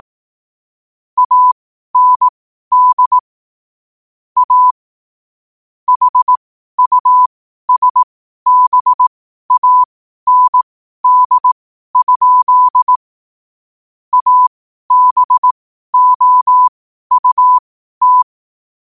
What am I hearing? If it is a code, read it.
Morse code, heard as a keyed tone, about 18 wpm, spaced as for 12 wpm, 1000 Hz. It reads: AND A HUSBAND? ABOUT
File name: qotdfiles / 12wpm